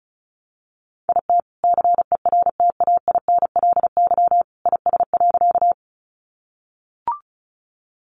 Morse code, heard as a keyed tone, about 34 wpm, 700 Hz. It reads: IT CERTAINLY IS.
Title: Sentences from Top 900 Words 35wpm